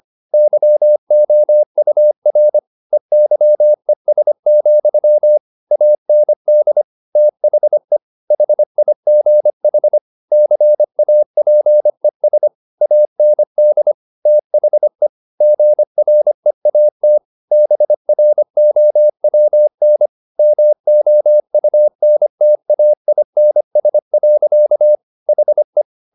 Morse code, heard as a keyed tone, about 25 wpm, 600 Hz. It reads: YOUR EYES, AND THE HIGH CAPES AND THE GREAT BROWN MOUNTAINS. HE
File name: the old man and the sea part 1 25wpm